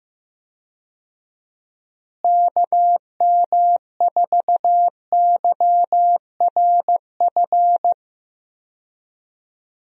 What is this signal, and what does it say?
Morse code, heard as a keyed tone, about 15 wpm, 700 Hz. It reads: KM4YRF
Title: USA Call Signs - 6 Characters - Repeated 3x - 15wpm